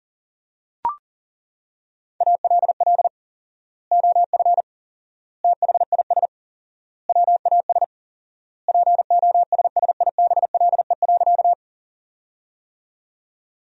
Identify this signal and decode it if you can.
Morse code, heard as a keyed tone, about 40 wpm, 700 Hz. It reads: ALL OF THIS WAS POSSIBLE.
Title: 4x Word-Spacing - Sentences from Top 500 Words - 40wpm